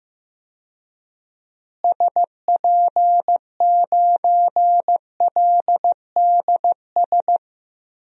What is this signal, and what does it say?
Morse code, heard as a keyed tone, about 15 wpm, 700 Hz. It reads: SP9LDS
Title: DXCC-100 Call Signs - 6 Characters - Repeated 2x - 15wpm